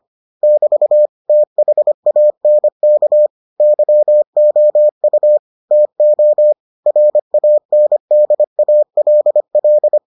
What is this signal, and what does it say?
Morse code, heard as a keyed tone, about 25 wpm, 600 Hz. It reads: <BT> THANK YOU TO RANDALL